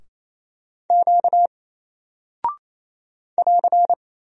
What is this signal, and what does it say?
Morse code, heard as a keyed tone, about 28 wpm, 700 Hz. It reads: Q <AR>
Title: Single Character (Letter:Number:Punctuation:Prosign) Alt 28wpm